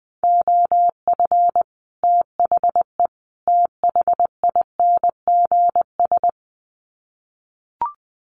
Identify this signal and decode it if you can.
Morse code, heard as a keyed tone, about 20 wpm, 700 Hz. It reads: OF THE THINGS
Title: Sets of 3 Words 20wpm